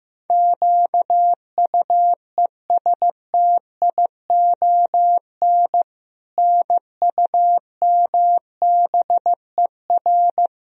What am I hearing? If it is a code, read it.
Morse code, heard as a keyed tone, about 16 wpm, 700 Hz. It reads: QUESTION NUMBER
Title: Sentences from Top 1000 Words - Review Part 1 - 15wpm